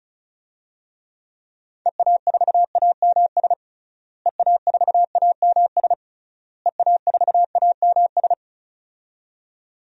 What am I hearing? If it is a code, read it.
Morse code, heard as a keyed tone, about 36 wpm, 700 Hz. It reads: EA4AMS EA4AMS EA4AMS
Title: DXCC-100 Call Signs - 6 Characters - Repeated 3x - 35wpm